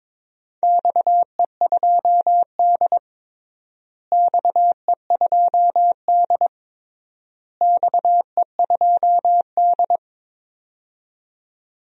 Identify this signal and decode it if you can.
Morse code, heard as a keyed tone, about 22 wpm, 700 Hz. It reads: XE2D XE2D XE2D
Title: DXCC-100 Call Signs - 4 Characters - Repeated 3x - 22wpm